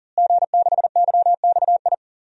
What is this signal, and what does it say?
Morse code, heard as a keyed tone, about 40 wpm, 700 Hz. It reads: G6YXI
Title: DXCC-100 Call Signs - 5 Characters - Repeated 3x - 40wpm